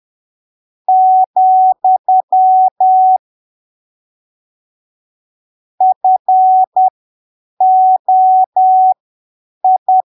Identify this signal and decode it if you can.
Morse code, heard as a keyed tone, about 10 wpm, spaced as for 7 wpm, 750 Hz. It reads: , FOI
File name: arquivos / texto / texto 14 10ppm